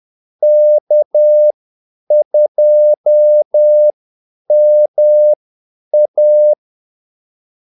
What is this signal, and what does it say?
Morse code, heard as a keyed tone, about 10 wpm, spaced as for 8 wpm, 600 Hz. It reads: K2MA
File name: Callsigns-10-8wpm